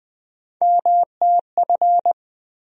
Morse code, heard as a keyed tone, about 20 wpm, 700 Hz. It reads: MTF
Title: USA Call Sign Postfixes - 3 Letters 20wpm